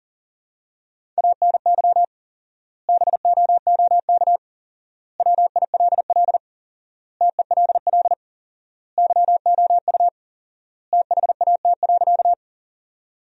Morse code, heard as a keyed tone, about 40 wpm, 700 Hz. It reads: ANY BOOK WILL TELL YOU THAT.
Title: 4x Word-Spacing - Sentences from Top 300 Words - 40wpm